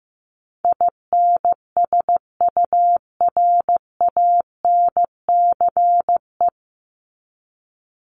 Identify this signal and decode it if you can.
Morse code, heard as a keyed tone, about 15 wpm, 700 Hz. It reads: INSURANCE
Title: Top 1001-1250 Words 15wpm